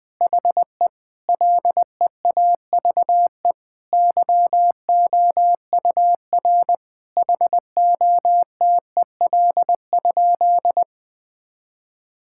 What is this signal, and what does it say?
Morse code, heard as a keyed tone, about 21 wpm, 700 Hz. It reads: HE LEAVE YOUR HOTEL?
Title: Sentences from Top 1600 Words 20wpm